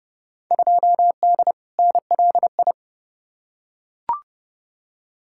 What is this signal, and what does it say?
Morse code, heard as a keyed tone, about 31 wpm, 700 Hz. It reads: 2D NLI
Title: Field Day Exchanges - 30wpm